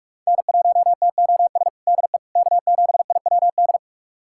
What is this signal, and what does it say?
Morse code, heard as a keyed tone, about 42 wpm, 700 Hz. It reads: N1TOS DE K7IWD
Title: USA Callsign DE Callsign - 5 Characters - 45wpm